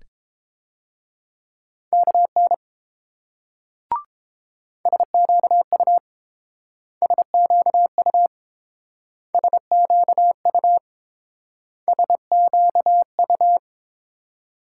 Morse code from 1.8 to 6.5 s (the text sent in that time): KN SQU